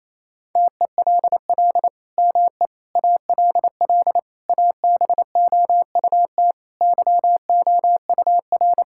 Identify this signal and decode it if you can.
Morse code, heard as a keyed tone, about 28 wpm, 700 Hz. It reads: TELL ME ALL ABOUT YOUR
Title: Sentences from Top 1000 Words - Review Part 6 - 28wpm